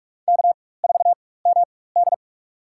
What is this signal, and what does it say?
Morse code, heard as a keyed tone, about 42 wpm, spaced as for 21 wpm, 700 Hz. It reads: K4MD
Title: USA Call Signs - 4x Character Spacing - 45wpm